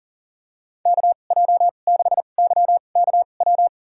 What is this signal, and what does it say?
Morse code, heard as a keyed tone, about 40 wpm, spaced as for 29 wpm, 700 Hz. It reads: KJ6YKW
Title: USA Call Signs - 2x Character Spacing - 40wpm